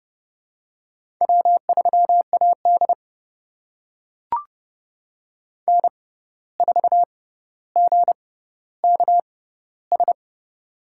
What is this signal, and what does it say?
Morse code, heard as a keyed tone, about 30 wpm, spaced as for 14 wpm, 700 Hz. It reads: W3AD N4GKS
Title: USA Call Signs - 6x Character Spacing - 30wpm